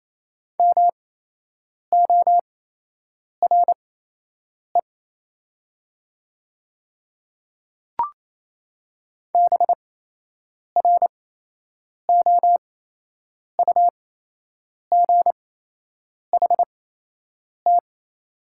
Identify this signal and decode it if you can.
Morse code, heard as a keyed tone, about 28 wpm, spaced as for 8 wpm, 690 Hz. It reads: MORE BROUGHT
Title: 8x Character Spacing - Top 500 Words - 28wpm